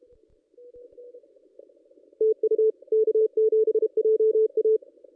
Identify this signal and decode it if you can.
Morse code, heard as a keyed tone, about 32 wpm, 440 Hz. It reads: TU K7JA
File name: contests / ss-cw-k5tr-op-n5rz -1800